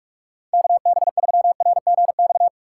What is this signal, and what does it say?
Morse code, heard as a keyed tone, about 46 wpm, 700 Hz. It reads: KB3RGX
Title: USA Call Signs - 3x Character Spacing - 45wpm